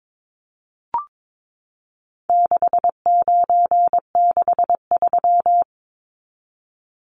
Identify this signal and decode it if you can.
Morse code, heard as a keyed tone, about 22 wpm, 700 Hz. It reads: 6963
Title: Numbers - 4-Digit 22wpm